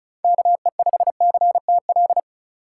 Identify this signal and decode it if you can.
Morse code, heard as a keyed tone, about 34 wpm, 700 Hz. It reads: KE5CTL
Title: USA Call Signs - 6 Characters - 4x Character Spacing - 35wpm